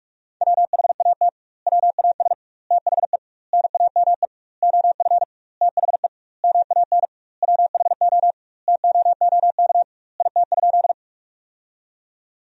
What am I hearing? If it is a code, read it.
Morse code, heard as a keyed tone, about 41 wpm, 700 Hz. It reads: WHAT WAS THE NAME OF THE MAN WHO TOOK IT?
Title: 2x Word-Spacing - Sentences from Top 300 Words - 45wpm